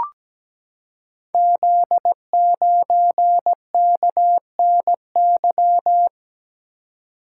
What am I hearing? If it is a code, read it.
Morse code, heard as a keyed tone, about 17 wpm, 700 Hz. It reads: Z9KNY
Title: Code Groups - 5 Letter:Numbers - 17wpm